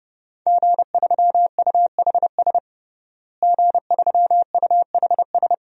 Morse code, heard as a keyed tone, about 30 wpm, 700 Hz. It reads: G3UHS G3UHS
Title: DXCC-100 Call Signs - 5 Characters - Repeated 2x - 30wpm